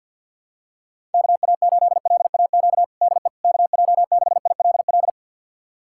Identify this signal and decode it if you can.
Morse code, heard as a keyed tone, about 50 wpm, 700 Hz. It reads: KA8LAQ DE KJ6ILL
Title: USA Callsign DE Callsign 50wpm